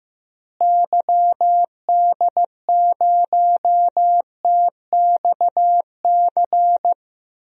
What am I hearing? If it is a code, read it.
Morse code, heard as a keyed tone, about 15 wpm, 700 Hz. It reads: YD0TXC
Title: DXCC-100 Call Signs - 6 Characters - Repeated 3x - 15wpm